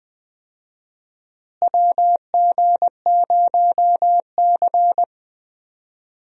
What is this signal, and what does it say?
Morse code, heard as a keyed tone, about 20 wpm, 700 Hz. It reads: WG0C
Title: USA Call Signs - 4 Characters - Repeated 2x - 20wpm